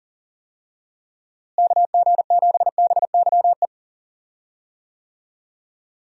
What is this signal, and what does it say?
Morse code, heard as a keyed tone, about 40 wpm, 700 Hz. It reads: KG7BYE
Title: USA Call Signs - 6 Characters - Repeated 2x - 40wpm